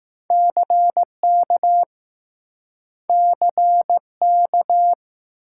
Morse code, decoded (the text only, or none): CK CK